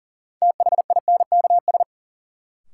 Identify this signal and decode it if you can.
Morse code, heard as a keyed tone, about 40 wpm, 700 Hz. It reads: THINKS
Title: Sentences from Top 1000 Words - Review Part 6 - New Word Drill - 40wpm